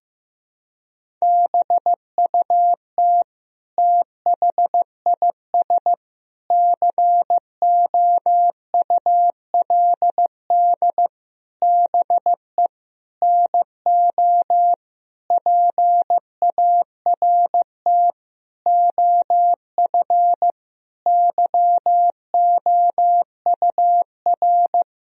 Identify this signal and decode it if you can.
Morse code, heard as a keyed tone, about 15 wpm, 700 Hz. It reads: BUT THIS COULD BE NO PART OF YOUR
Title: Sentences from Top 1000 Words - Review Part 2 - 15wpm